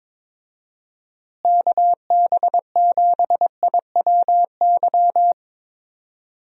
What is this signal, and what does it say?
Morse code, heard as a keyed tone, about 22 wpm, 700 Hz. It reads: KB7IWY
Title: USA Call Signs - 6 Characters - Repeated 3x - 22wpm